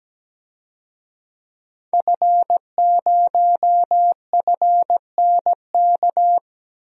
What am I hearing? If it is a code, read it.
Morse code, heard as a keyed tone, about 17 wpm, 700 Hz. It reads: F0FNK
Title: DXCC-100 Call Signs - 5 Characters - 17wpm